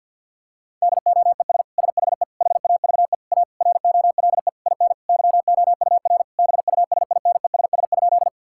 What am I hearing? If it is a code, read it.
Morse code, heard as a keyed tone, about 50 wpm, 700 Hz. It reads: DOES SHE HAVE A ROLE IN YOUR BUSINESS?
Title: Sentences from Top 1000 Words - Review Part 6 - 50wpm